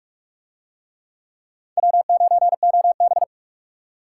Morse code, heard as a keyed tone, about 44 wpm, 700 Hz. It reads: W9OB